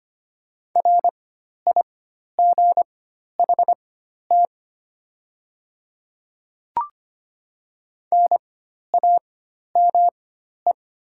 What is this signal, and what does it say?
Morse code, heard as a keyed tone, about 25 wpm, spaced as for 12 wpm, 700 Hz. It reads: RIGHT NAME
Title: 4x Character Spacing - Top 200 Words - 25wpm